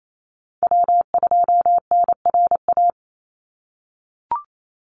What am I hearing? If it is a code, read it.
Morse code, heard as a keyed tone, about 28 wpm, 700 Hz. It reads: W2NRA
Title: Call Signs - 28wpm